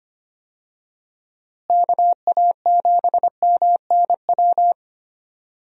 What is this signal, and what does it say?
Morse code, heard as a keyed tone, about 25 wpm, 700 Hz. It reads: KA7MNW
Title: USA Call Signs - 6 Characters - Repeated 3x - 25wpm